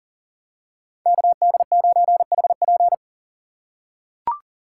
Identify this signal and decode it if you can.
Morse code, heard as a keyed tone, about 40 wpm, 700 Hz. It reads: KD9HP
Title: USA Call Signs - 5 Characters - 2x Character Spacing - 40wpm